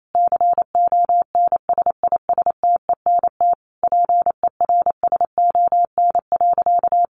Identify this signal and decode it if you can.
Morse code, heard as a keyed tone, about 28 wpm, 700 Hz. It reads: CONSISTENT PERSON.
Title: Sentences from Top 1500 Words 28wpm